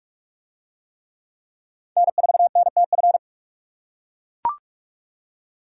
Morse code, heard as a keyed tone, about 44 wpm, 700 Hz. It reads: N4NTF